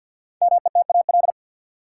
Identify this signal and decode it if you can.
Morse code, heard as a keyed tone, about 48 wpm, 700 Hz. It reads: METAL
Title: Sentences from Top 1000 Words - Review Part 6 - New Word Drill - 50wpm